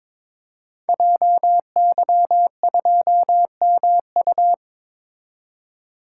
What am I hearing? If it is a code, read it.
Morse code, heard as a keyed tone, about 22 wpm, 700 Hz. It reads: JY2MU
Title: Code Groups - 5 Letter:Numbers - 22wpm